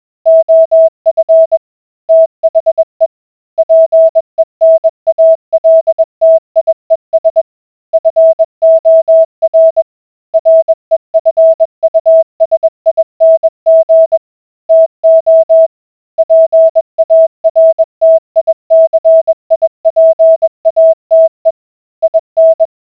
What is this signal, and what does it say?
Morse code, heard as a keyed tone, about 21 wpm, 650 Hz. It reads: OF THE PENALTIES FOR REFUSING TO PARTICIPATE IN